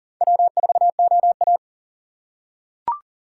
Morse code, heard as a keyed tone, about 37 wpm, 700 Hz. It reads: W4OA
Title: USA Call Signs - 4x Character Spacing - 40wpm